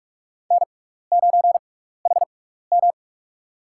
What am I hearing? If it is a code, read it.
Morse code, heard as a keyed tone, about 46 wpm, spaced as for 15 wpm, 700 Hz. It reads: N9HM
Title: USA Call Signs - 4 Characters - 6x Character Spacing - 45wpm